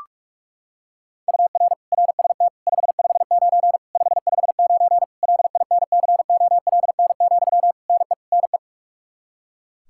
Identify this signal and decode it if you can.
Morse code, heard as a keyed tone, about 41 wpm, 700 Hz. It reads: UR RST 559 559 LINCOLN, NE NE
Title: Practice QSO - Exchange 1 - Beginning - 45wpm